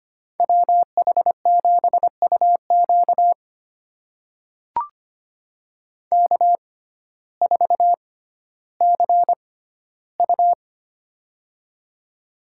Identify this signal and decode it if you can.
Morse code, heard as a keyed tone, about 25 wpm, spaced as for 14 wpm, 700 Hz. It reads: WH7UQ K4CU
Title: USA Call Signs - 6x Character Spacing - 25wpm